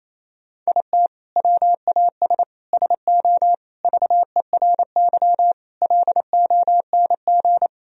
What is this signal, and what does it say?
Morse code, heard as a keyed tone, about 28 wpm, 700 Hz. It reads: IT WAS SO VERY LONG
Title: Sentences from Top 1000 Words - Review Part 3 - 28wpm